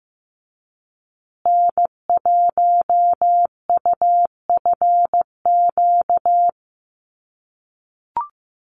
Morse code, heard as a keyed tone, about 15 wpm, 700 Hz. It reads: N1UFQ
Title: USA Call Signs Repeated 3x 15wpm